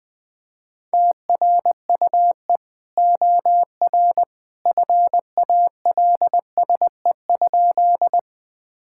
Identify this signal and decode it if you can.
Morse code, heard as a keyed tone, about 21 wpm, 700 Hz. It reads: TRUE OR FALSE?
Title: Sentences from Top 1600 Words 20wpm